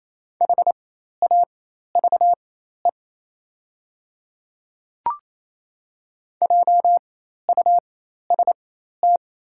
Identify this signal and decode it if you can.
Morse code, heard as a keyed tone, about 28 wpm, spaced as for 13 wpm, 700 Hz. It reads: HAVE JUST
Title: 4x Character Spacing - Top 200 Words - 28wpm